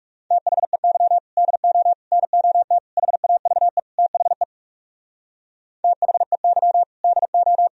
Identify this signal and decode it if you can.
Morse code, heard as a keyed tone, about 40 wpm, 700 Hz. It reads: THEY DO NOT HAVE THE THEY DO